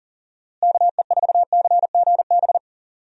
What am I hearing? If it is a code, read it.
Morse code, heard as a keyed tone, about 37 wpm, 700 Hz. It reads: KE4CGB
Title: USA Call Signs - 6 Characters - Repeated 3x - 40wpm